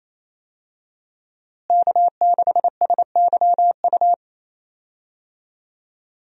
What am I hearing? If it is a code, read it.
Morse code, heard as a keyed tone, about 28 wpm, 700 Hz. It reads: K6SYU
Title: USA Call Signs - 5 Characters - Repeated 2x - 28wpm